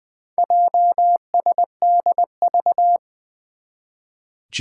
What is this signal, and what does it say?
Morse code, heard as a keyed tone, about 20 wpm, 700 Hz. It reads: JSDV
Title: Four Challenging Letter-Number Combinations 20wpm